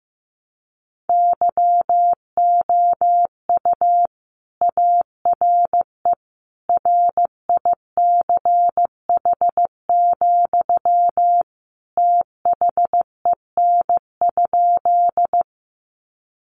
Morse code, heard as a keyed tone, about 15 wpm, 700 Hz. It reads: YOU ARE RICH, THEN?
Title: Sentences from Top 1100 Words 15wpm